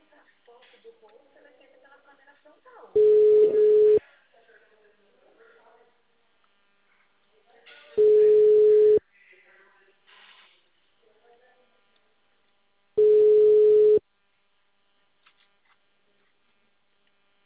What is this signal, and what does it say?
Ringback tone (Europe, 425 Hz), with speech.
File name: out- -2005- - - .75